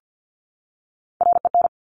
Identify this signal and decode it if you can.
Morse code, heard as a keyed tone, about 50 wpm, 700 Hz. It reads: REN
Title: Top Three Letter Combinations in Words 50wpm